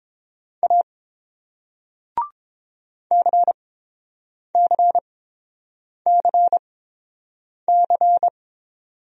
Morse code, heard as a keyed tone, about 31 wpm, spaced as for 7 wpm, 700 Hz. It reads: A CCCC